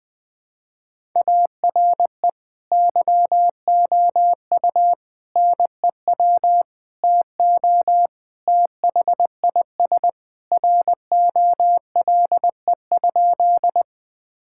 Morse code, heard as a keyed tone, about 20 wpm, 700 Hz. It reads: ARE YOU NEW TO THIS ROLE?
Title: Sentences from Top 1000 Words - Review Part 6 - 20wpm